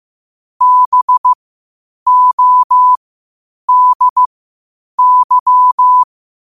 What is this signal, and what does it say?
Morse code, heard as a keyed tone, about 15 wpm, spaced as for 8 wpm, 1000 Hz. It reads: BODY
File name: qotdfiles / 8wpm